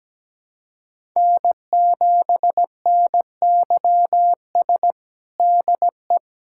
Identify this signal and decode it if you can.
Morse code, heard as a keyed tone, about 17 wpm, 700 Hz. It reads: N7NYS DE
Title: USA Callsign DE Callsign 17wpm